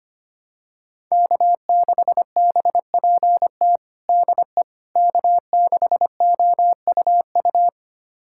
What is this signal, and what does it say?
Morse code, heard as a keyed tone, about 25 wpm, 700 Hz. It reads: K6BPT DE K6OUU
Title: USA Callsign DE Callsign - 5 Characters - 25wpm